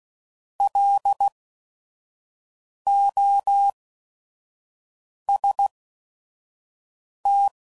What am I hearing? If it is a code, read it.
Morse code, heard as a keyed tone, about 16 wpm, spaced as for 5 wpm, 800 Hz. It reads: LOST